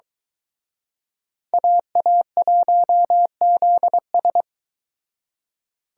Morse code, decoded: AA1ZS

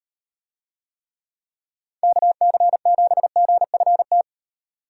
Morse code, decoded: KC7GFT

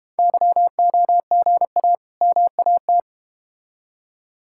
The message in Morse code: YOGA MAT

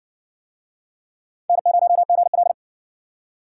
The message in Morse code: N0GL

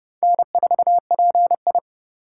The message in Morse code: N4PI

